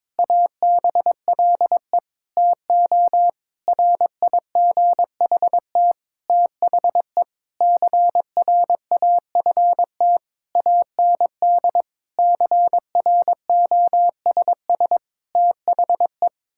Morse code, heard as a keyed tone, about 22 wpm, 700 Hz. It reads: ABLE TO RIGHT THE CRAFT AND CROSS THE
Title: Sentences from Top 1000 Words - Review Part 6 - 22wpm